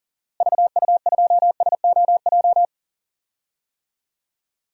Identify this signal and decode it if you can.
Morse code, heard as a keyed tone, about 38 wpm, 700 Hz. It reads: VU2SOJ
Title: DXCC-100 Call Signs - 6 Characters - Repeated 2x - 40wpm